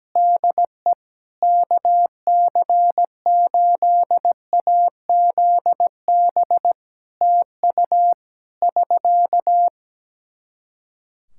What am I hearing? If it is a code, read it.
Morse code, heard as a keyed tone, about 17 wpm, 700 Hz. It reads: DE KC8AZB TU <SK>
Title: Practice QSO - Exchange 4 - Ending - 17wpm